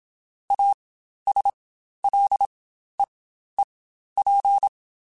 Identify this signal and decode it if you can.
Morse code, heard as a keyed tone, about 27 wpm, spaced as for 12 wpm, 800 Hz. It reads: ASLEEP
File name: Quotes Spaced / Quotes Spaced-mp3s / ChildAsleep sp @25